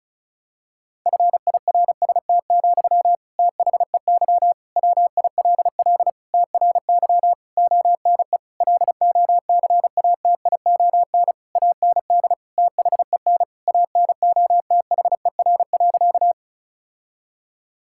Morse code, heard as a keyed tone, about 34 wpm, 700 Hz. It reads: FIRST, THEY WILL TRY ONE LOCATION AND THEN ANOTHER.
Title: Sentences from Top 1000 Words - Review Part 6 - 35wpm